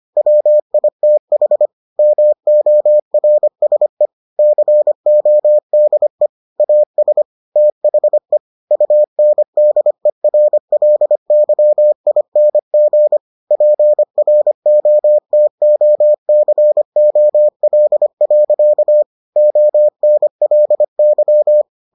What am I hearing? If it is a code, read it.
Morse code, heard as a keyed tone, about 25 wpm, 600 Hz. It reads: WITH MORSE CODE AS THE UNDERLYING PROTOCOL. ONLY